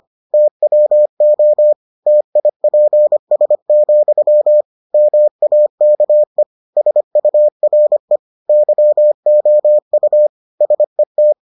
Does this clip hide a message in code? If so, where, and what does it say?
Yes: Morse code from 0.3 to 11.3 s, heard as a keyed tone, about 25 wpm, 600 Hz. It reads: TWO TIPS, MAKE SURE YOU SET